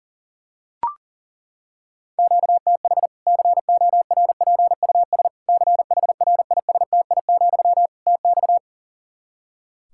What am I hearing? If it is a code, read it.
Morse code, heard as a keyed tone, about 40 wpm, 700 Hz. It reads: QTH CORPUS CHRISTI, TX
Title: K1USN SST QSO Practice 40wpm